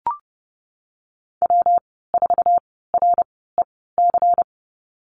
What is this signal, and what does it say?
Morse code, heard as a keyed tone, about 30 wpm, spaced as for 17 wpm, 700 Hz. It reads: W4REC
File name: Lesson 19g - Letter w - Call Signs 30wpm